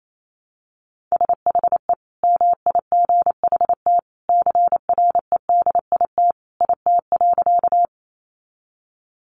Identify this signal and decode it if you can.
Morse code, heard as a keyed tone, about 28 wpm, 700 Hz. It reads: SHE MIGHT CREDIT IT.
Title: Sentences from Top 1000 Words 28wpm